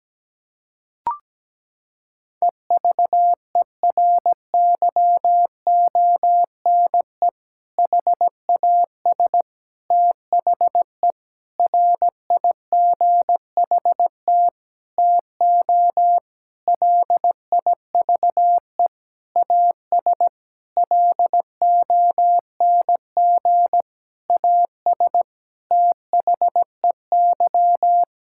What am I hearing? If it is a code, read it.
Morse code, heard as a keyed tone, about 17 wpm, 700 Hz. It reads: EVERYONE HAS THE RIGHT TO LIVE AS LONG AS THEY
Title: Sentences from Top 1000 Words - Review Part 5 - 17wpm